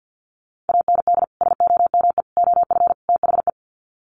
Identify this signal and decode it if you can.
Morse code, heard as a keyed tone, about 50 wpm, 700 Hz. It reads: AND SOME OF THE
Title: Sets of 4 Words 50wpm